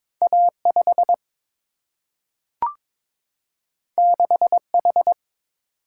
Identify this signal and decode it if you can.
Morse code, heard as a keyed tone, about 22 wpm, 700 Hz. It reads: A 5 6 H
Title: Single-Letter Single-Number 22wpm